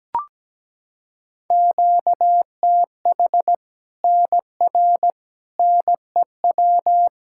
Morse code, heard as a keyed tone, about 17 wpm, 700 Hz. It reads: QTH NR NEW
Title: K1USN SST QSO Practice 17wpm